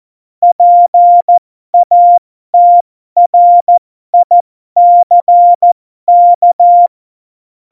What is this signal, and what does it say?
Morse code, heard as a keyed tone, about 14 wpm, spaced as for 12 wpm, 700 Hz. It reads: PATRICK
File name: Morsekurs / Witze